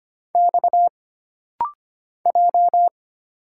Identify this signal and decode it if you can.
Morse code, heard as a keyed tone, about 25 wpm, 700 Hz. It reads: X J
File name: Single Character (Letter:Number:Punctuation:Prosign) - Rapid-Fire - 25wpm